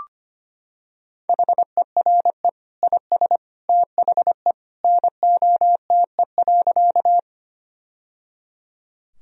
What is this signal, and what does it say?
Morse code, heard as a keyed tone, about 25 wpm, 700 Hz. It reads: HERE IS THE NOTE.
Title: Sentences from Top 1000 Words - Review Part 2 - 25wpm